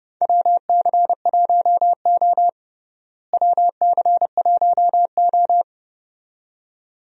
Morse code, heard as a keyed tone, about 30 wpm, 700 Hz. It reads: WC1O WC1O
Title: USA Call Signs - 4 Characters - Repeated 2x - 30wpm